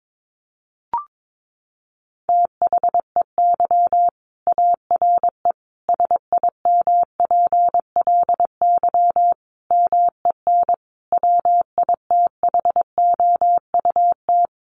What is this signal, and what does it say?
Morse code, heard as a keyed tone, about 22 wpm, 700 Hz. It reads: THEY ARE SIMPLY MEN WITHOUT
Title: Sentences from Top 1500 Words 22wpm